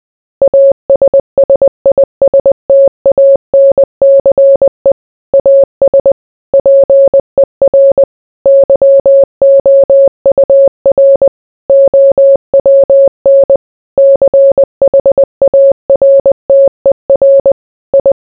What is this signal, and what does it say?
Morse code, heard as a keyed tone, about 20 wpm, 550 Hz. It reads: ASSISTANCE AS PER YOUR OWN CHARTER I